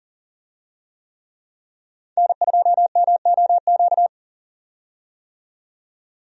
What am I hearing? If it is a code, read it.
Morse code, heard as a keyed tone, about 40 wpm, 700 Hz. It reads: N2MOQ
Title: USA Call Signs - 5 Characters - 40wpm